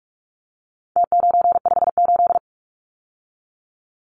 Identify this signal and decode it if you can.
Morse code, heard as a keyed tone, about 44 wpm, 700 Hz. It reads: T958
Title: Numbers - Contest Style - 4-Digit 45wpm